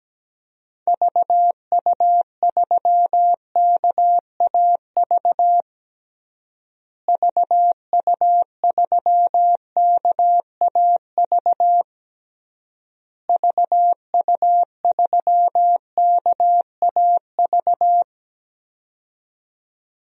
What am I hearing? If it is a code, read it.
Morse code, heard as a keyed tone, about 17 wpm, 700 Hz. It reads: VU3KAV VU3KAV VU3KAV
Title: DXCC-100 Call Signs - 6 Characters - Repeated 3x - 17wpm